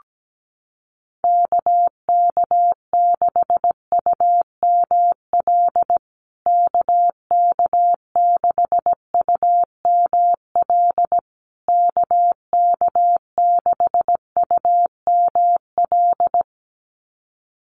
Morse code, heard as a keyed tone, about 17 wpm, 700 Hz. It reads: KK6UML KK6UML KK6UML
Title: USA Call Signs Repeated 3x 17wpm